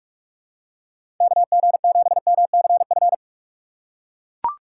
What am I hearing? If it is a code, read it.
Morse code, heard as a keyed tone, about 42 wpm, 700 Hz. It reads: KG7MCF
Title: USA Call Signs - 6x Character Spacing - 45wpm